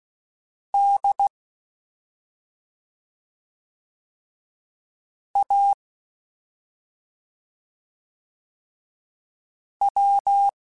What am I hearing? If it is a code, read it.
Morse code, heard as a keyed tone, about 16 wpm, spaced as for 2 wpm, 800 Hz. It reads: DAW
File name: Special INGs-mp3s / Special INGs-mp3s / Slow ing @10